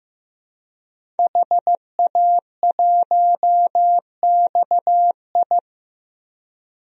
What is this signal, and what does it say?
Morse code, heard as a keyed tone, about 15 wpm, 700 Hz. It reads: HA1XI
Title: DXCC-100 Call Signs - 5 Characters - Repeated 3x - 15wpm